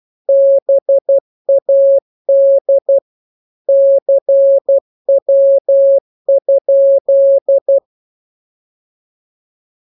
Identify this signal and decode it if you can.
Morse code, heard as a keyed tone, about 12 wpm, 550 Hz. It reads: BAD CW?
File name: jokes / joke 8 12 0000